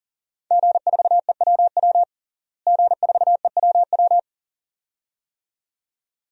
Morse code, heard as a keyed tone, about 40 wpm, 700 Hz. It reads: G4EWW G4EWW
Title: DXCC-100 Call Signs - 5 Characters - Repeated 2x - 40wpm